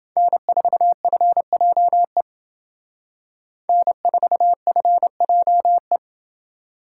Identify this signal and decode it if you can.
Morse code, heard as a keyed tone, about 29 wpm, 700 Hz. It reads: N4FJE N4FJE